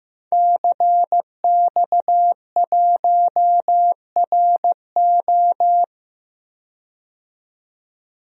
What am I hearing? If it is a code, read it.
Morse code, heard as a keyed tone, about 15 wpm, 700 Hz. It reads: CX1RO
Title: DXCC-100 Call Signs - 5 Characters - 15wpm